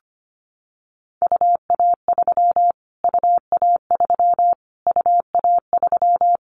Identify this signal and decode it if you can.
Morse code, heard as a keyed tone, about 25 wpm, 700 Hz. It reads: UA3 UA3 UA3
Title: Callsign Prefixes for Top 25 DXCC - Repeated 3x - 25wpm